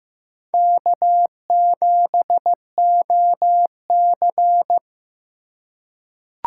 Morse code, heard as a keyed tone, about 15 wpm, 700 Hz. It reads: K7OC
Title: USA Call Signs - 4 Characters - 3x Character Spacing - 15wpm